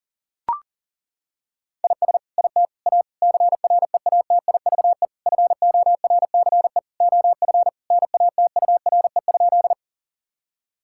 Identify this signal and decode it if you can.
Morse code, heard as a keyed tone, about 37 wpm, 700 Hz. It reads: IS IT A CREATIVE FORCE OF NATURE?
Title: Sentences from Top 1000 Words - Review Part 5 - 40wpm